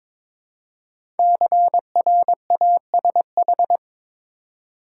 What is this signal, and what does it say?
Morse code, heard as a keyed tone, about 23 wpm, 700 Hz. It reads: CRASH
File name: 5 Letter Words - First 3 Letters Given - 22wpm